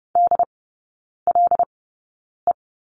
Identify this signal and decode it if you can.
Morse code, heard as a keyed tone, about 30 wpm, spaced as for 9 wpm, 700 Hz. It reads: DLE
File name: Lesson 19e - Letter w - Three Characters 30wpm